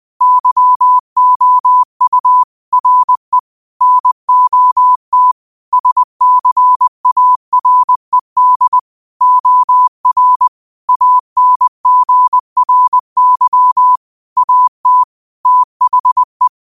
Morse code, heard as a keyed tone, about 20 wpm, 1000 Hz. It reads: YOU RE NOT SCARED OR ANGRY AT THE
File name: qotdfiles / 20wpm